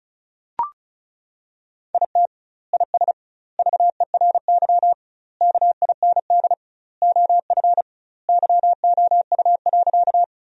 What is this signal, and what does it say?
Morse code, heard as a keyed tone, about 34 wpm, 700 Hz. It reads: IT IS VERY KIND OF YOU.
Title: 2x Word-Spacing - Sentences from Top 200 Words - 35wpm